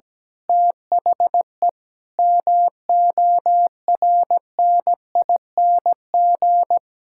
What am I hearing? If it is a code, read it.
Morse code, heard as a keyed tone, about 17 wpm, 700 Hz. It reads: THE MORNING
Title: Pangram Challenge v2 - 17wpm